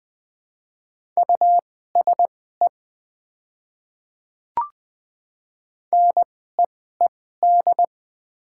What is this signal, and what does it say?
Morse code, heard as a keyed tone, about 20 wpm, spaced as for 14 wpm, 700 Hz. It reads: USE NEED
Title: 2x Character Spacing - Top 200 Words - 20wpm